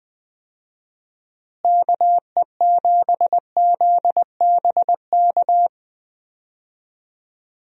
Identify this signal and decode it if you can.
Morse code, heard as a keyed tone, about 20 wpm, 700 Hz. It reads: KE7ZBK
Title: USA Call Signs - 6 Characters - Repeated 3x - 20wpm